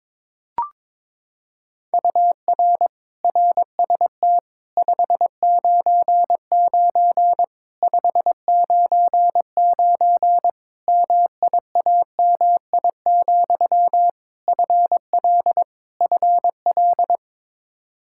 Morse code, heard as a keyed tone, about 22 wpm, 700 Hz. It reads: UR RST 599 599 MIAMI, FL FL
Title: Practice QSO - Exchange 1 - Beginning - 22wpm